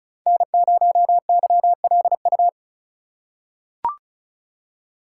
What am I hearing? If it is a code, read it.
Morse code, heard as a keyed tone, about 36 wpm, 700 Hz. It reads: N0YLU
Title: USA Call Signs - 5 Characters - 4x Character Spacing - 35wpm